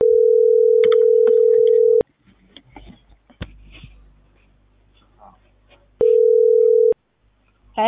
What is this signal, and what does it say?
Ringback tone (North America), 440 + 480 Hz.